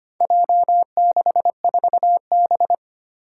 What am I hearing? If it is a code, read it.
Morse code, heard as a keyed tone, about 25 wpm, 700 Hz. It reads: J64B
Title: Four Challenging Letter-Number Combinations 25wpm